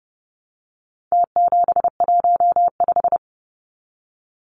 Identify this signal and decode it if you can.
Morse code, heard as a keyed tone, about 30 wpm, 700 Hz. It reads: T715
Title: Numbers - Contest Style - 4-Digit 30wpm